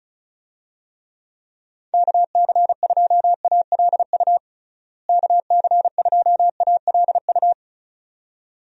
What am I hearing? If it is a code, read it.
Morse code, heard as a keyed tone, about 36 wpm, 700 Hz. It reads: KC2ALU KC2ALU
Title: USA Call Signs - 6 Characters - Repeated 2x - 35wpm